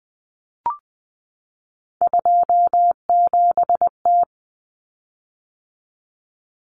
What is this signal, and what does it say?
Morse code, heard as a keyed tone, about 20 wpm, 700 Hz. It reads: 27T
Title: Common Cut Numbers - 3-Digit 20wpm